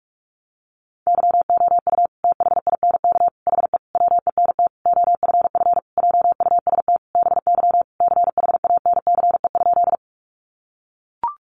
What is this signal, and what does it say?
Morse code, heard as a keyed tone, about 44 wpm, 700 Hz. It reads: YOU THINK HE WENT OFF JUST BY CHANCE?
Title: Sentences from Top 800 Words 45wpm